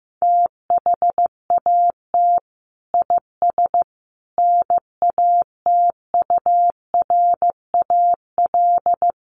Morse code, heard as a keyed tone, about 15 wpm, 700 Hz. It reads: THAT IS NATURAL
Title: Sentences from Top 700 Words 15wpm